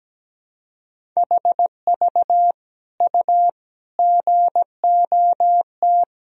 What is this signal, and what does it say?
Morse code, heard as a keyed tone, about 17 wpm, 700 Hz. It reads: HV U GOT
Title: Sentences from Top 300 Words and Abbreviations 17wpm